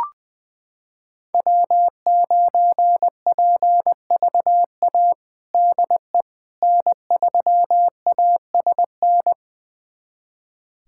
Morse code, heard as a keyed tone, about 21 wpm, 700 Hz. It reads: W9PVA DE N3ASN
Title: USA Callsign DE Callsign - 5 Characters - 20wpm